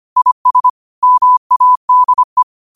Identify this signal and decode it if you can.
Morse code, heard as a keyed tone, about 25 wpm, 1000 Hz. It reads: IS MADE
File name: qotdfiles / 25wpm